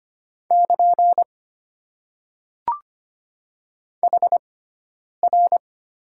Morse code, heard as a keyed tone, about 25 wpm, spaced as for 9 wpm, 690 Hz. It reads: <KN> HR
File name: Top 90 QSO Elements - 6x Character Spacing - 25wpm